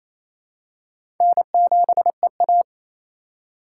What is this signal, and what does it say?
Morse code, heard as a keyed tone, about 28 wpm, 700 Hz. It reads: N7EA
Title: USA Call Signs - 4 Characters - 6x Character Spacing - 28wpm